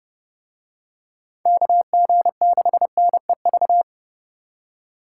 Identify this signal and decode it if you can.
Morse code, heard as a keyed tone, about 30 wpm, 700 Hz. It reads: KG6NEV